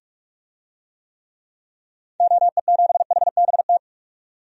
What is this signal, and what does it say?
Morse code, heard as a keyed tone, about 47 wpm, 700 Hz. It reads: OE7HBT
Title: DXCC-100 Call Signs - 6 Characters - 45wpm